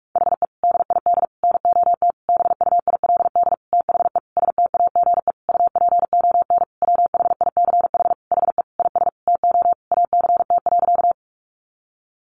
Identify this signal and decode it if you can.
Morse code, heard as a keyed tone, about 44 wpm, 700 Hz. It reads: HE DID NOT BUILD THE STAGE UPON WHICH HE IS TO ACT.
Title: Sentences from Top 1200 Words 45wpm